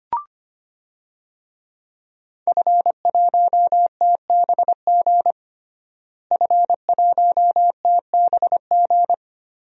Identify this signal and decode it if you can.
Morse code, heard as a keyed tone, about 25 wpm, 700 Hz. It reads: F1TBG F1TBG
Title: DXCC-100 Call Signs - 5 Characters - Repeated 3x - 25wpm